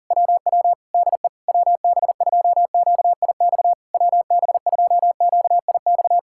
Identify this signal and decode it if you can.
Morse code, heard as a keyed tone, about 40 wpm, 700 Hz. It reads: WW DE WB2QIX WB2QIX